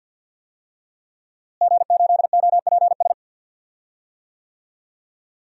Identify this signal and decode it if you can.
Morse code, heard as a keyed tone, about 51 wpm, 700 Hz. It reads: G8OPS
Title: DXCC-100 Call Signs - 5 Characters - Repeated 2x - 50wpm